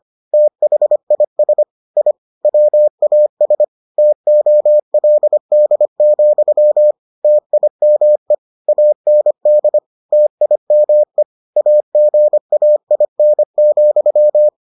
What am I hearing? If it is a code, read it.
Morse code, heard as a keyed tone, about 25 wpm, 600 Hz. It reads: THIS I WAS TOLD, TIME AND TIME AGAIN,